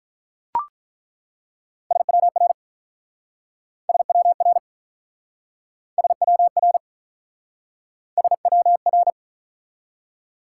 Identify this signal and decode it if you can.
Morse code, heard as a keyed tone, about 47 wpm, 700 Hz. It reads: SWR SWR SWR SWR